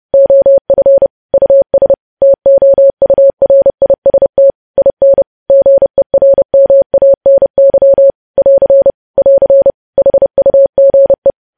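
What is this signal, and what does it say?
Morse code, heard as a keyed tone, about 30 wpm, 550 Hz. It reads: OF US TOURIST IN GERMANY <AR> <AR> HUGE
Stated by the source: Morse code news headlines practice at 30 WPM.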